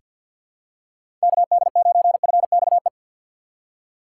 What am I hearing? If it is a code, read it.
Morse code, heard as a keyed tone, about 47 wpm, 700 Hz. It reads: KD9FXE
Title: USA Call Signs - 4x Character Spacing - 50wpm